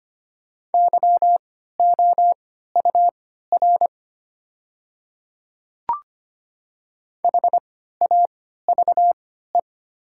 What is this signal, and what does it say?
Morse code, heard as a keyed tone, about 25 wpm, spaced as for 14 wpm, 700 Hz. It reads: YOUR HAVE
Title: 3x Character Spacing - Top 100 Words - 25wpm